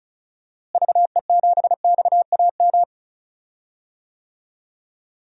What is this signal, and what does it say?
Morse code, heard as a keyed tone, about 36 wpm, 700 Hz. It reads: VE7XAM
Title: DXCC-100 Call Signs - 6 Characters - Repeated 3x - 35wpm